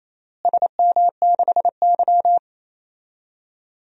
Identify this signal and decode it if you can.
Morse code, heard as a keyed tone, about 27 wpm, 700 Hz. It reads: SM6Y
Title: DXCC-100 Call Signs - 4 Characters - Repeated 3x - 28wpm